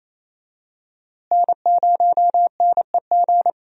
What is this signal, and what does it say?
Morse code, heard as a keyed tone, about 28 wpm, 700 Hz. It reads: N0NEG
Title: USA Call Signs - 5 Characters - Repeated 3x - 28wpm